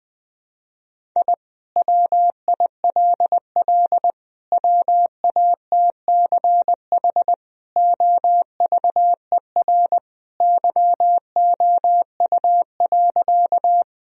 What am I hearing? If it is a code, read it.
Morse code, heard as a keyed tone, about 20 wpm, 700 Hz. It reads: I WILL WATCH OVER YOU.
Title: Sentences from Top 1000 Words - Review Part 1 - 20wpm